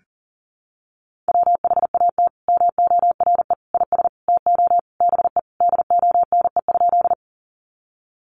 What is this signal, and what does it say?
Morse code, heard as a keyed tone, about 40 wpm, 700 Hz. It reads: WHAT MORE IS TO BE DONE?
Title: Sentences from Top 500 Words 40wpm